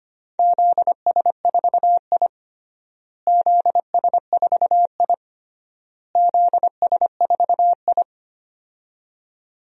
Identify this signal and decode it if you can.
Morse code, heard as a keyed tone, about 25 wpm, 700 Hz. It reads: ZS4I ZS4I ZS4I
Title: DXCC-100 Call Signs - 4 Characters - Repeated 3x - 25wpm